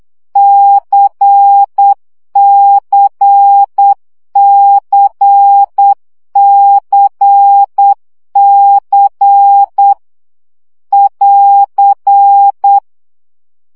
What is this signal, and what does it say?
Morse code, heard as a keyed tone, about 8 wpm, 800 Hz. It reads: CCCCC <AR>
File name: CW gyakorlas c 7 baud